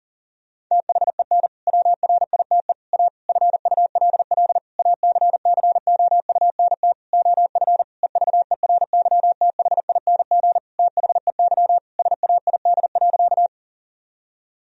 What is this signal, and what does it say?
Morse code, heard as a keyed tone, about 40 wpm, 700 Hz. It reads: THEN WRITE A FULL ACCOUNT OF EVERYTHING THEY SAID.
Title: Sentences from Top 1000 Words - Review Part 3 - 40wpm